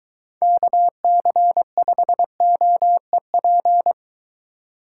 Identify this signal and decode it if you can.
Morse code, heard as a keyed tone, about 22 wpm, 700 Hz. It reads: KC5OEP